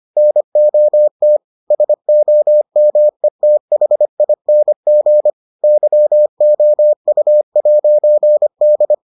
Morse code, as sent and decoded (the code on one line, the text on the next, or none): -. --- - / ... --- -- . - .... .. -. --. / -.-- --- ..- .----. -..
NOT SOMETHING YOU'D